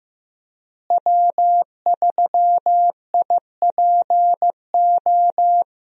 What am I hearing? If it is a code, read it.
Morse code, heard as a keyed tone, about 15 wpm, 700 Hz. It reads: W3IPO
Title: USA Call Signs - 5 Characters - Repeated 3x - 15wpm